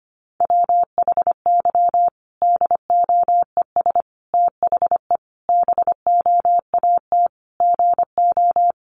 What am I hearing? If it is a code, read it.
Morse code, heard as a keyed tone, about 25 wpm, 700 Hz. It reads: WHY DOES THE BOAT GO
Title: Sentences from Top 500 Words 25wpm